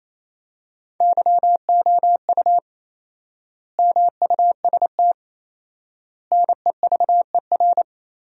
Morse code, heard as a keyed tone, about 28 wpm, 700 Hz. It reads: YOU MUST NEVER
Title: 4x Word-Spacing - Sentences from Top 300 Words - 28wpm